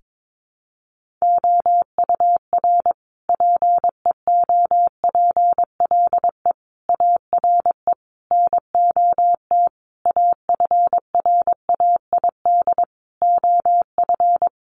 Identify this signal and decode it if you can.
Morse code, heard as a keyed tone, about 22 wpm, 700 Hz. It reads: OUR PEOPLE ARE NOT AFRAID OF
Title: Sentences from Top 1500 Words 22wpm